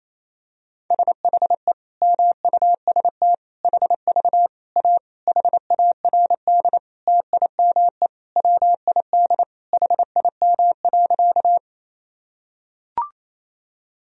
Morse code, heard as a keyed tone, about 28 wpm, 700 Hz. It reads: SHE MUST HV A HARD TIME WID HIM.
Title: Sentences from Top 300 Words and Abbreviations 28wpm